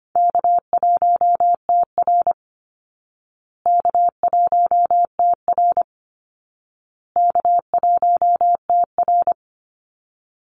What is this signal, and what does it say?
Morse code, heard as a keyed tone, about 25 wpm, 700 Hz. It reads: K1TR K1TR K1TR
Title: Easy Call Signs Repeated 3x - 25wpm